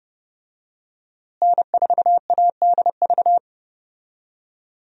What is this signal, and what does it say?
Morse code, heard as a keyed tone, about 32 wpm, 700 Hz. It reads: N4ADV
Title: USA Call Signs - 5 Characters - Repeated 3x - 30wpm